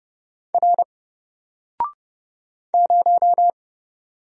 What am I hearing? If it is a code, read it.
Morse code, heard as a keyed tone, about 31 wpm, 700 Hz. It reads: R 0
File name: Single-Letter-Number-30wpm